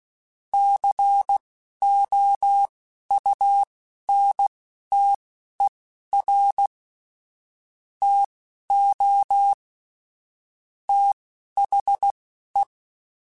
Morse code, heard as a keyed tone, about 16 wpm, spaced as for 12 wpm, 800 Hz. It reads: COUNTER TO THE